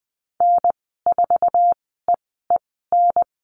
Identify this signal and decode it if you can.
Morse code, heard as a keyed tone, about 20 wpm, spaced as for 15 wpm, 700 Hz. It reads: N4EEN
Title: 088 Lesson 15g - Number 5 - Call Signs 20wpm